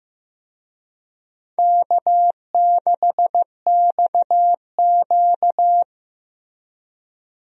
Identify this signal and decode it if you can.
Morse code, heard as a keyed tone, about 15 wpm, 700 Hz. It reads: K6XQ
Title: USA Call Signs - 4 Characters - Repeated 3x - 15wpm